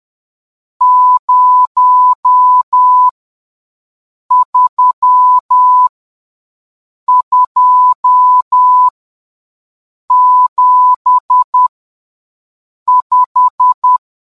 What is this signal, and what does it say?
Morse code, heard as a keyed tone, about 10 wpm, spaced as for 5 wpm, 1000 Hz. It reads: 03275